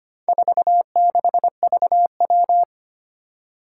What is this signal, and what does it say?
Morse code, heard as a keyed tone, about 25 wpm, 700 Hz. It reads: 46VW
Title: Four Challenging Letter-Number Combinations 25wpm